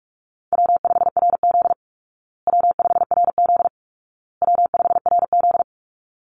Morse code, heard as a keyed tone, about 44 wpm, 700 Hz. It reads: W5RZ W5RZ W5RZ
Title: Easy Call Signs Repeated 3x - 45wpm